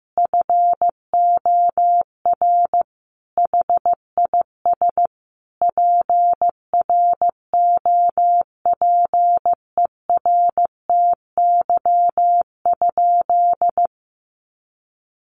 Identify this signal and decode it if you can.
Morse code, heard as a keyed tone, about 15 wpm, 700 Hz. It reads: FOR HIS PROPERTY?
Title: Sentences from Top 1500 Words 15wpm